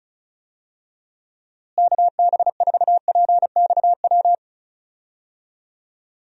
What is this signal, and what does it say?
Morse code, heard as a keyed tone, about 36 wpm, 700 Hz. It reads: KB4PXW
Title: USA Call Signs - 6 Characters - Repeated 2x - 35wpm